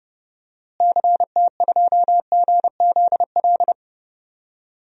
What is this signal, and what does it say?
Morse code, heard as a keyed tone, about 30 wpm, 700 Hz. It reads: CT2GZL
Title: DXCC-100 Call Signs - 6 Characters - Repeated 2x - 30wpm